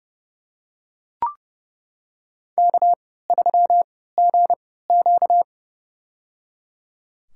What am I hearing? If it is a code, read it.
Morse code, heard as a keyed tone, about 29 wpm, spaced as for 17 wpm, 700 Hz. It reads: K3GQ
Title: USA Call Signs - 3x Character Spacing - 30wpm